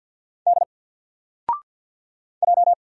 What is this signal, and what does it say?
Morse code, heard as a keyed tone, about 50 wpm, 700 Hz. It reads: D J
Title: Single Letter-Number Alt 50wpm